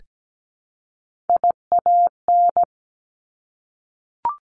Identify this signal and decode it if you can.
Morse code, heard as a keyed tone, about 17 wpm, 700 Hz. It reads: IAN
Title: Top Three Letter Combinations in Words 17wpm